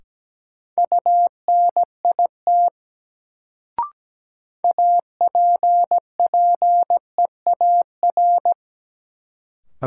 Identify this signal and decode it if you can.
Morse code, heard as a keyed tone, about 17 wpm, 700 Hz. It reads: UNIT APPEAR
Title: Sentences from Top 1000 Words - Review Part 2 - New Word Drill - 17wpm